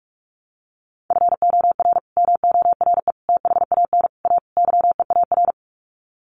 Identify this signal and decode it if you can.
Morse code, heard as a keyed tone, about 44 wpm, 700 Hz. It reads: FOR MORE THAN A YEAR
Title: Sets of 5 Words 45wpm